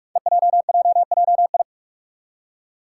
Morse code, heard as a keyed tone, about 46 wpm, 700 Hz. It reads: EJJJI
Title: Five Challenging Letter-Number Combinations 45wpm